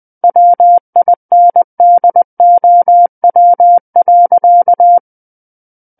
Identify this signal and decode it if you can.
Morse code, heard as a keyed tone, about 20 wpm, 700 Hz. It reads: WINDOW.
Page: SKCC Learning Center - Morse Code Practice Files